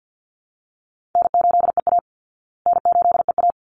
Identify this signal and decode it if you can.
Morse code, heard as a keyed tone, about 49 wpm, 700 Hz. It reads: N8EA N8EA
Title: Call Signs Repeated 3x - 50wpm